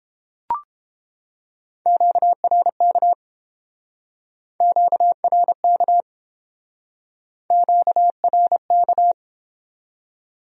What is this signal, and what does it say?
Morse code, heard as a keyed tone, about 31 wpm, 700 Hz. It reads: QRK QRK QRK